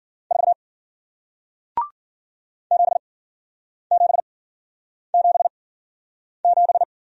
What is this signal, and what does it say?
Morse code, heard as a keyed tone, about 53 wpm, spaced as for 9 wpm, 700 Hz. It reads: 4 7777